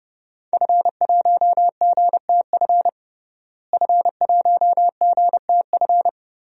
Morse code, heard as a keyed tone, about 30 wpm, 700 Hz. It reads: F1GTF F1GTF
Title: DXCC-100 Call Signs - 5 Characters - Repeated 2x - 30wpm